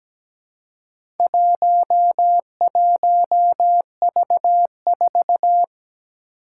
Morse code, heard as a keyed tone, about 17 wpm, 700 Hz. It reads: 11V4
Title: Four Challenging Letter-Number Combinations 17wpm